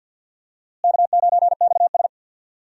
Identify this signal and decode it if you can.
Morse code, heard as a keyed tone, about 51 wpm, 700 Hz. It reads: K9XS
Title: USA Call Signs - 4 Characters - Repeated 3x - 50wpm